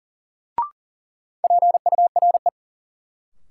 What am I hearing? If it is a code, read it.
Morse code, heard as a keyed tone, about 40 wpm, 700 Hz. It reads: PURE
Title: Sentences from Top 1500 Words - New Word Drill - 40wpm